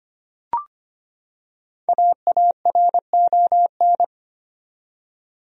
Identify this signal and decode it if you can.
Morse code, heard as a keyed tone, about 25 wpm, 700 Hz. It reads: AARON
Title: USA Common Names - Max 5 Letters - 25wpm